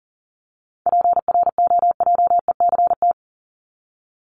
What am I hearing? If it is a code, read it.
Morse code, heard as a keyed tone, about 40 wpm, 700 Hz. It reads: PROJECT
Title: Top 1251-1500 Words 40wpm